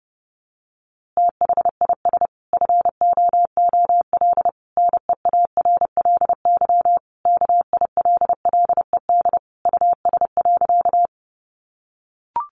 Morse code, heard as a keyed tone, about 28 wpm, 700 Hz. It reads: THIS FOOL NEARLY KILLED US.
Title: A Sound Of Thunder 30wpm